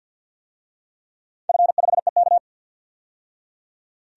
Morse code, heard as a keyed tone, about 51 wpm, 700 Hz. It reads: F5EK
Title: DXCC-100 Call Signs - 4 Characters - 50wpm